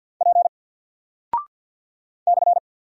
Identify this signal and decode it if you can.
Morse code, heard as a keyed tone, about 50 wpm, 700 Hz. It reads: P /
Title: Single Character (Letter:Number:Punctuation:Prosign) 50wpm